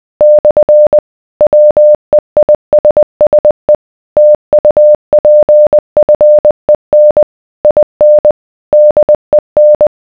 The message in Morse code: / WEISSE TUPFEN IN DEN